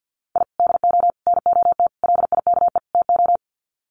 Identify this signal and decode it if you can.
Morse code, heard as a keyed tone, about 50 wpm, 700 Hz. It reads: I DO NOT LIKE TO
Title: Sets of 5 Words 50wpm